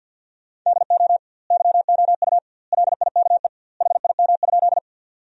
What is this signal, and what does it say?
Morse code, heard as a keyed tone, about 50 wpm, 700 Hz. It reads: DO YOU LIKE HIM?
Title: 2x Word-Spacing - Sentences from Top 100 Words - 50wpm